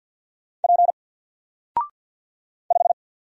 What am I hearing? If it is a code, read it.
Morse code, heard as a keyed tone, about 49 wpm, 700 Hz. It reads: P 5
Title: Single Character (Letter:Number:Punctuation:Prosign) Alt 50wpm